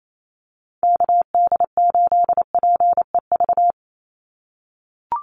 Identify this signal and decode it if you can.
Morse code, heard as a keyed tone, about 28 wpm, 700 Hz. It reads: KD8PEV
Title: USA Call Signs 28wpm